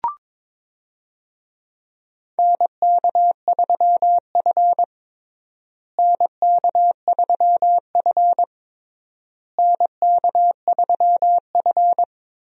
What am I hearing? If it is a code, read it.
Morse code, heard as a keyed tone, about 22 wpm, 700 Hz. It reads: NK3F NK3F NK3F
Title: USA Call Signs - 4 Characters - Repeated 3x - 22wpm